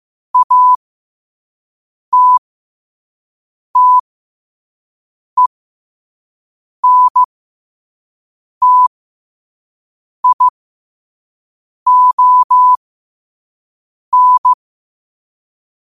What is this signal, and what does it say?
Morse code, heard as a keyed tone, about 15 wpm, spaced as for 5 wpm, 1000 Hz. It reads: ATTENTION
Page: Quotes for Wed, 13 Aug 2025 in Morse Code at 5 words per minute.